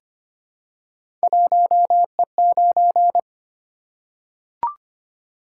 Morse code, heard as a keyed tone, about 25 wpm, 700 Hz. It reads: 1E9
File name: Three Letter-Number Combinations - 25wpm